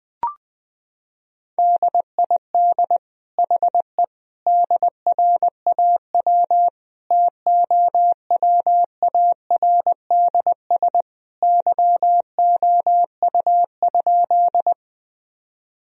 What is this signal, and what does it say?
Morse code, heard as a keyed tone, about 21 wpm, 700 Hz. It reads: DID HE DRAW TOWARDS YOU?
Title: Sentences from Top 1000 Words - Review Part 2 - 20wpm